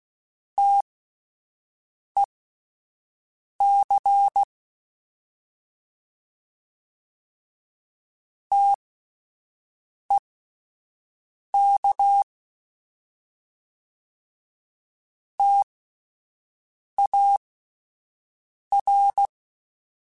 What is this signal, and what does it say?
Morse code, heard as a keyed tone, about 16 wpm, spaced as for 5 wpm, 800 Hz. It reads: TEC TEK TAR